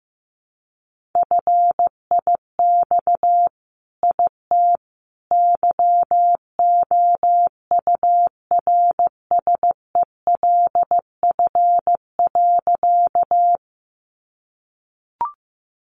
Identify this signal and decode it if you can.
Morse code, heard as a keyed tone, about 15 wpm, 700 Hz. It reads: FIX IT YOURSELF.
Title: Sentences from Top 1400 Words 15wpm